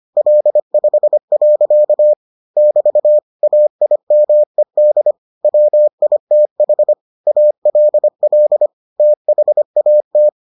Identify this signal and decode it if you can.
Morse code, heard as a keyed tone, about 25 wpm, 600 Hz. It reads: L5. <BT> AIMED WITH ALL THAT